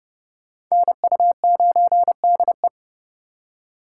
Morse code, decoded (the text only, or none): NU9DE